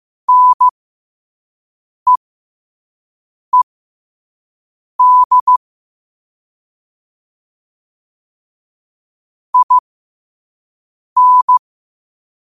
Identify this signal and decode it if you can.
Morse code, heard as a keyed tone, about 15 wpm, spaced as for 5 wpm, 1000 Hz. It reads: NEED IN
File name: qotdfiles / 5wpm